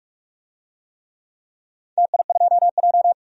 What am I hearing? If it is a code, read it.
Morse code, heard as a keyed tone, about 42 wpm, 700 Hz. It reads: TI2J
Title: DXCC-100 Call Signs - 4 Characters - 45wpm